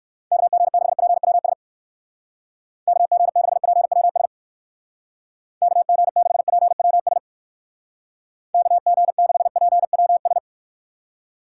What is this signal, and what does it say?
Morse code, heard as a keyed tone, about 62 wpm, 700 Hz. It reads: KG6PWS KG6PWS KG6PWS KG6PWS